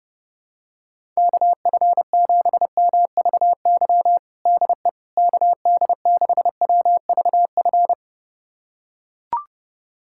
Morse code, heard as a keyed tone, about 30 wpm, 700 Hz. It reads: KF7MVY DE KD6WVF
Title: USA Callsign DE Callsign - 6 Characters - 30wpm